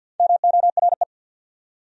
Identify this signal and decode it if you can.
Morse code, heard as a keyed tone, about 54 wpm, 700 Hz. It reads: MORE
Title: Top 100 Words 50wpm